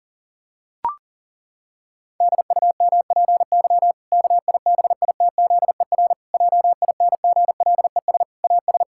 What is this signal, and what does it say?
Morse code, heard as a keyed tone, about 40 wpm, 700 Hz. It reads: DUMPY KIBITZER JINGLES AS
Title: Pangram Challenge v2 - 40wpm